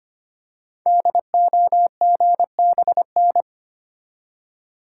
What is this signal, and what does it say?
Morse code, heard as a keyed tone, about 25 wpm, 700 Hz. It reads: DOGBN
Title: Code Groups - 5 Letters - 25wpm